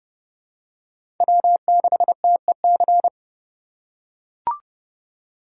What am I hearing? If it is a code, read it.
Morse code, heard as a keyed tone, about 30 wpm, 700 Hz. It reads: W6TEC
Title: USA Call Signs - 5 Characters - 4x Character Spacing - 30wpm